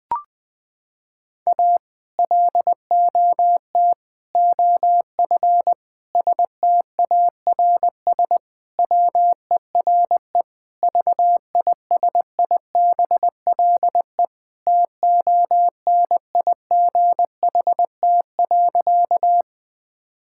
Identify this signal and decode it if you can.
Morse code, heard as a keyed tone, about 20 wpm, 700 Hz. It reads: A LOT OF STARS WERE VISIBLE TONIGHT.
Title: Sentences with 2-Word Phrases - 20wpm